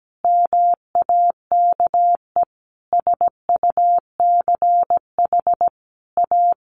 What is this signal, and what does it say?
Morse code, heard as a keyed tone, about 17 wpm, 700 Hz. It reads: MAKE SUCH A
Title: Sentences from Top 1200 Words 17wpm